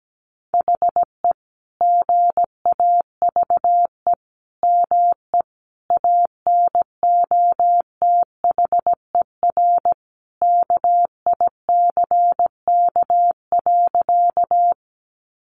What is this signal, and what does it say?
Morse code, heard as a keyed tone, about 17 wpm, 700 Hz. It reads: HE GAVE ME ANOTHER KICK.
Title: Sentences from Top 1300 Words 17wpm